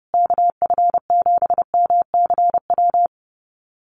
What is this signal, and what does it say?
Morse code, heard as a keyed tone, about 30 wpm, 700 Hz. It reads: KF7MCW
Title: USA Call Signs Repeated 2x 30wpm